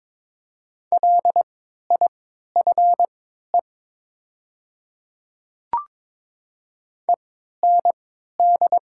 Morse code, heard as a keyed tone, about 22 wpm, spaced as for 13 wpm, 700 Hz. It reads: LIFE END
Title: 3x Character Spacing - Top 300 Words - 22wpm